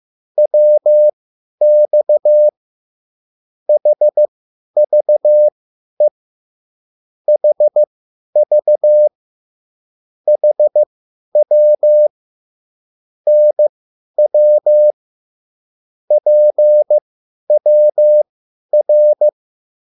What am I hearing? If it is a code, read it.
Morse code, heard as a keyed tone, about 15 wpm, spaced as for 10 wpm, 600 Hz. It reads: WX HVE HV HW NW PWR
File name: QSL / CW lesson